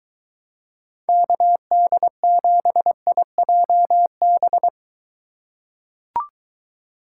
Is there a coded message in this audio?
Yes: Morse code from 1.1 to 4.7 s, heard as a keyed tone, about 22 wpm, 700 Hz. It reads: KD7IJB